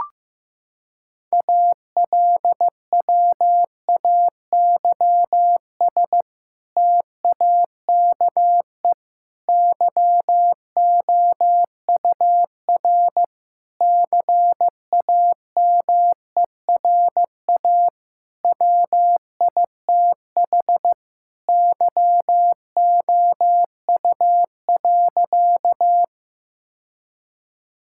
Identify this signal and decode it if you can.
Morse code, heard as a keyed tone, about 15 wpm, 700 Hz. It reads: ALWAYS TAKE YOUR CAMERA WITH YOU.
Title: Sentences from Top 1000 Words - Review Part 5 - 15wpm